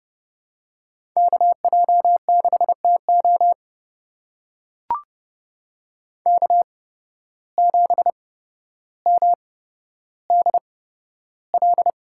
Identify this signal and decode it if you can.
Morse code, heard as a keyed tone, about 30 wpm, spaced as for 13 wpm, 700 Hz. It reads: KJ6TO K7MDL
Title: USA Call Signs - 5 Characters - 8x Character Spacing - 30wpm